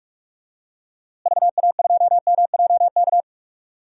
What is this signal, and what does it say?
Morse code, heard as a keyed tone, about 41 wpm, 700 Hz. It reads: VA2MJK
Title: DXCC-100 Call Signs - 6 Characters - Repeated 3x - 45wpm